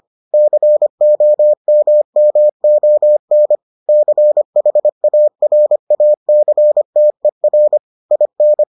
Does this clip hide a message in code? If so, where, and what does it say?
Yes: Morse code from 0.3 to 8.6 s, heard as a keyed tone, about 25 wpm, 600 Hz. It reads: COMMON CHARACTER IN